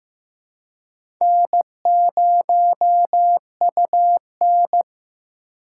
Morse code, heard as a keyed tone, about 15 wpm, 700 Hz. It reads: N0UN
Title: USA Call Signs - 4 Characters - 4x Character Spacing - 15wpm